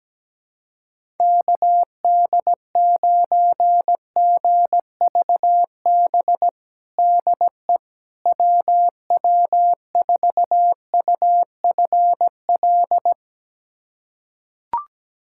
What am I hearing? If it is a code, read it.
Morse code, heard as a keyed tone, about 17 wpm, 700 Hz. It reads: KD9GVB DE WW4UFL
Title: USA Callsign DE Callsign - 6 Characters - 17wpm